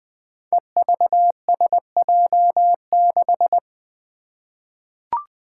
Morse code, heard as a keyed tone, about 20 wpm, 700 Hz. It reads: EVSJ6
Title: Five Challenging Letter-Number Combinations 20wpm